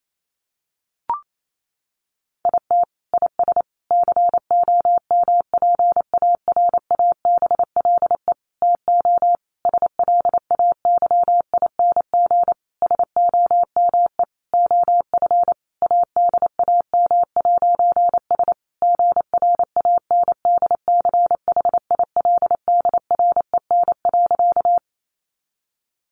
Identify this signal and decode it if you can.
Morse code, heard as a keyed tone, about 28 wpm, 700 Hz. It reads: IT IS COMPARABLE TO SLAYING SOME OF ADAM'S GRANDCHILDREN.
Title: A Sound Of Thunder 28wpm